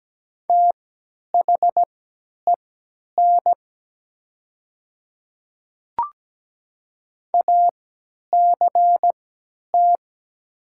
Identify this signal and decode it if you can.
Morse code, heard as a keyed tone, about 17 wpm, spaced as for 10 wpm, 700 Hz. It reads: THEN ACT
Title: 3x Character Spacing - Top 300 Words - 17wpm